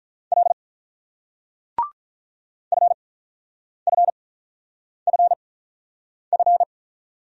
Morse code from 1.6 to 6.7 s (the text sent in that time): FFFF